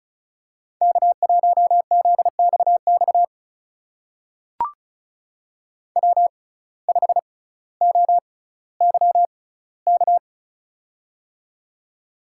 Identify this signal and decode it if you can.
Morse code, heard as a keyed tone, about 36 wpm, spaced as for 18 wpm, 700 Hz. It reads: K1ZXX W5OYK
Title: USA Call Signs - 6x Character Spacing - 35wpm